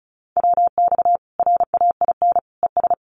Morse code, heard as a keyed tone, about 34 wpm, 700 Hz. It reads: WX RAIN ES
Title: Common QSO Phrases 35wpm